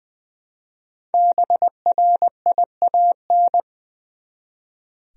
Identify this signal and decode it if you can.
Morse code, heard as a keyed tone, about 20 wpm, 700 Hz. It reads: BRIAN
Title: USA Common Names - 20wpm